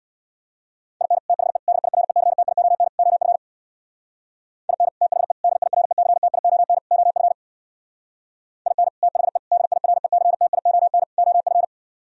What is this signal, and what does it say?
Morse code, heard as a keyed tone, about 64 wpm, 700 Hz. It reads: IN THE DIRECTION OF IN THE DIRECTION OF IN THE DIRECTION OF